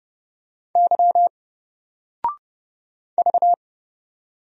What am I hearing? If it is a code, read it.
Morse code, heard as a keyed tone, about 29 wpm, 700 Hz. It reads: Y V
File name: Single-Letter-Number-30wpm